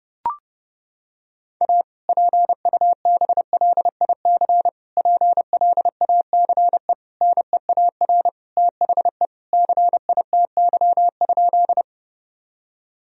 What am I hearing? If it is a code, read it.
Morse code, heard as a keyed tone, about 32 wpm, 700 Hz. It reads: A PUBLIC PLACE NEAR THE CITY?
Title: Sentences from Top 1000 Words - Review Part 3 - 30wpm